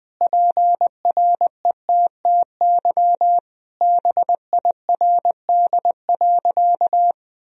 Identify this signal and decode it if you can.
Morse code, heard as a keyed tone, about 20 wpm, 700 Hz. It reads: PRETTY BIRD.
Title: Sentences from Top 1000 Words - Review Part 2 - 20wpm